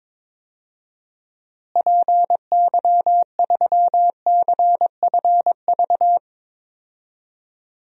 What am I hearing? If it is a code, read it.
Morse code, heard as a keyed tone, about 21 wpm, 700 Hz. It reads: PY3CFV
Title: DXCC-100 Call Signs - 6 Characters - Repeated 2x - 22wpm